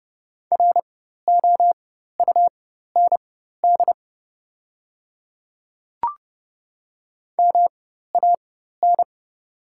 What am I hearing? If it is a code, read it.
Morse code, heard as a keyed tone, about 29 wpm, spaced as for 14 wpm, 700 Hz. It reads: ROUND MAN
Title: 4x Character Spacing - Top 500 Words - 30wpm